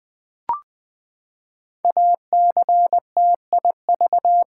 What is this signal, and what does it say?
Morse code, heard as a keyed tone, about 20 wpm, 700 Hz. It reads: ACTIV
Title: Build A Word - 5 Letters At A Time - 10-Letter Words - 20wpm